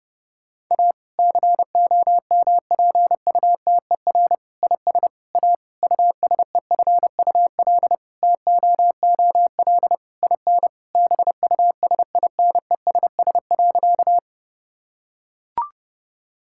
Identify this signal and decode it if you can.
Morse code, heard as a keyed tone, about 30 wpm, 700 Hz. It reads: A COMPUTER IS A USEFUL TOOL IN BUSINESS.
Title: Sentences from Top 1000 Words - Review Part 6 - 30wpm